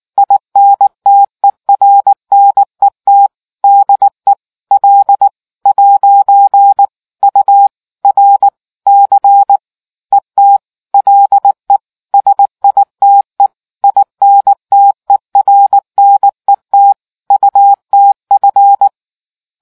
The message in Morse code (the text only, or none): INTERNET DE L'URC ET LE SITE INTERNET UTF